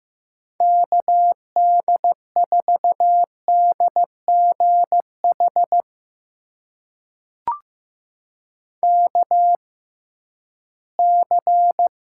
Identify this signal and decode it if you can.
Morse code, heard as a keyed tone, about 15 wpm, spaced as for 11 wpm, 700 Hz. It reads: KD4DGH KC
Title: USA Call Signs - 6 Characters - 6x Character Spacing - 15wpm